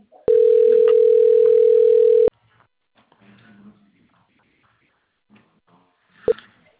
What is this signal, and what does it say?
Ringback tone (North America), 440 + 480 Hz.
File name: out- - - - - .96793